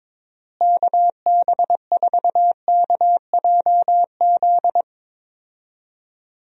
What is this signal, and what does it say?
Morse code, heard as a keyed tone, about 22 wpm, 700 Hz. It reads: KB4KJZ
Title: USA Call Signs - 6 Characters - Repeated 3x - 22wpm